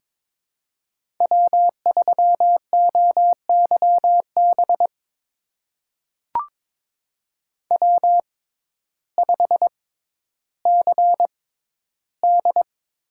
Morse code, heard as a keyed tone, about 21 wpm, spaced as for 12 wpm, 700 Hz. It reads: W3OYB W5CD
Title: USA Call Signs - 5 Characters - 6x Character Spacing - 22wpm